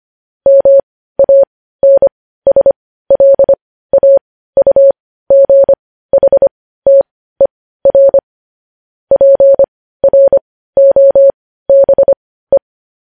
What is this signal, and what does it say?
Morse code, heard as a keyed tone, about 25 wpm, spaced as for 15 wpm, 550 Hz. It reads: MANSLAUGHTER PROBE
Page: news-headlines-in-morse-code-at-15-wpm